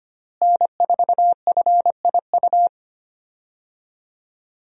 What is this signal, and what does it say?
Morse code, heard as a keyed tone, about 25 wpm, 700 Hz. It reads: N4FIU
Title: USA Call Signs - 5 Characters - Repeated 2x - 25wpm